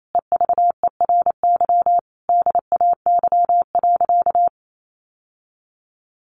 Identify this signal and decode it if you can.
Morse code, heard as a keyed tone, about 28 wpm, 700 Hz. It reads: EVERY DAY.
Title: Sentences from Top 700 Words 28wpm